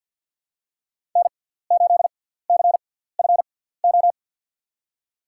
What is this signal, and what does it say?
Morse code, heard as a keyed tone, about 48 wpm, spaced as for 17 wpm, 700 Hz. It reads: N8CFO
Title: USA Call Signs - 5 Characters - 6x Character Spacing - 50wpm